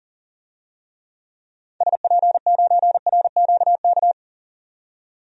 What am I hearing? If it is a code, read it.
Morse code, heard as a keyed tone, about 40 wpm, 700 Hz. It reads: SP9RQK
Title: DXCC-100 Call Signs - 6 Characters - 40wpm